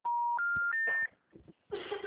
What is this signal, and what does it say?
Special information tone, with noise.
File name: out- -2003- - - .32